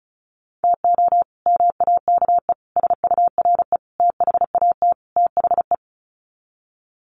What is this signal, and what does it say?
Morse code, heard as a keyed tone, about 34 wpm, 700 Hz. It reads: TO MAKE SURE THAT THE
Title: Sets of 5 Words 35wpm